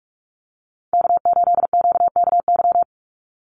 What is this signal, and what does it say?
Morse code, heard as a keyed tone, about 44 wpm, 700 Hz. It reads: K8QKY
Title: Call Signs - 45wpm